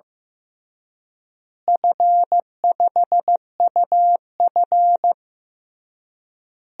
Morse code, heard as a keyed tone, about 15 wpm, 700 Hz. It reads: F5UF
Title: DXCC-100 Call Signs - 4 Characters - Repeated 3x - 15wpm